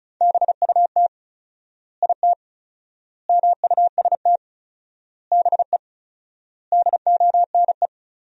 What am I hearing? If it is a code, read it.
Morse code, heard as a keyed tone, about 36 wpm, 700 Hz. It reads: BUT IT MUST BE DONE
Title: 4x Word-Spacing - Sentences from Top 500 Words - 35wpm